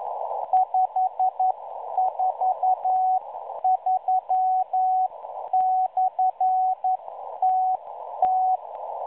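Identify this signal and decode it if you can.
Morse code, heard as a keyed tone, about 11 wpm, spaced as for 9 wpm, 750 Hz. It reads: 543/TT
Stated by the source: Morse signal received on 5463.0 kHz via a WebSDR receiver, catalogued as numbers station M03.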